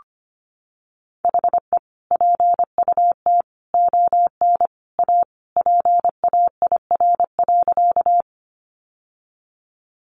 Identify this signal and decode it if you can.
Morse code, heard as a keyed tone, about 25 wpm, 700 Hz. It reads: HE PUT ON A PAIR.
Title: Sentences from Top 1200 Words 25wpm